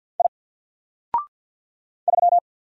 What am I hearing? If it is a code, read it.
Morse code, heard as a keyed tone, about 50 wpm, 700 Hz. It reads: I 3
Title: Single Letter-Number Alt 50wpm